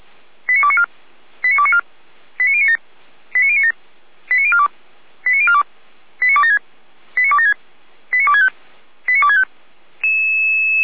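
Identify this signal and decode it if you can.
Five-tone selective call (ZVEI), heard as quick runs of five tones, tones of about 70 ms. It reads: 89294 89294 89097 89097 89042 89042 89276 89276 89265 89265
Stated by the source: ZVEI five-tone call 89140 received on an SDR.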